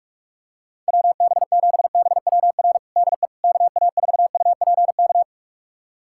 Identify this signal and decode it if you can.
Morse code, heard as a keyed tone, about 41 wpm, 700 Hz. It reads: WB7BWR DE KA4UPK
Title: USA Callsign DE Callsign 45wpm